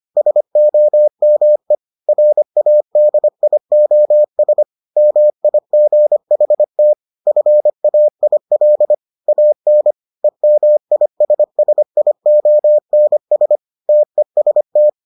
Morse code, heard as a keyed tone, about 25 wpm, 600 Hz. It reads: SOME RADIOS MIGHT FAIL AN EMISSIONS TEST